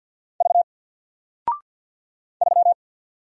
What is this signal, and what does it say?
Morse code, heard as a keyed tone, about 48 wpm, 700 Hz. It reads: V 3
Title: Single Character (Letter:Number:Punctuation:Prosign) 50wpm